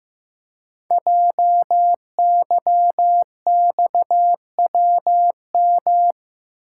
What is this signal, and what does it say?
Morse code, heard as a keyed tone, about 15 wpm, 700 Hz. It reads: JYXWM
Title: Code Groups - 5 Letters - 15wpm